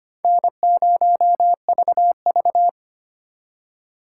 Morse code, heard as a keyed tone, about 34 wpm, 700 Hz. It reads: N0VV